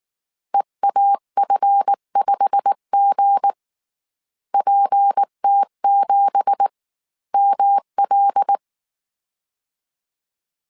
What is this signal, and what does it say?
Morse code, heard as a keyed tone, about 19 wpm, spaced as for 16 wpm, 800 Hz. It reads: EAF5G PT7ML
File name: CW / DET / 1 9 1500C S B 18wpm 2551